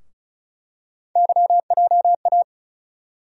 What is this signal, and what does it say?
Morse code, heard as a keyed tone, about 36 wpm, 700 Hz. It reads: YJA
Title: USA Call Sign Postfixes - 3 Letters 35wpm